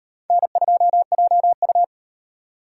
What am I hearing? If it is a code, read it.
Morse code, heard as a keyed tone, about 37 wpm, 700 Hz. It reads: N2JU